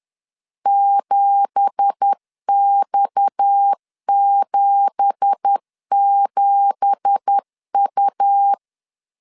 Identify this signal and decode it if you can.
Morse code, heard as a keyed tone, about 10 wpm, 800 Hz. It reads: 7X77U
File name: LEZIONI / 9 2867 10wpm 2733